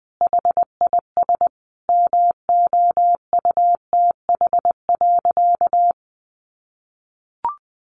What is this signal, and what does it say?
Morse code, heard as a keyed tone, about 20 wpm, 700 Hz. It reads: HIS MOUTH.
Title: A Sound Of Thunder 20wpm